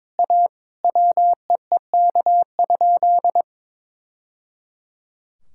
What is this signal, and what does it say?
Morse code, heard as a keyed tone, about 22 wpm, 700 Hz. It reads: A WEEK?
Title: Sentences from Top 1000 Words - Review Part 6 - 22wpm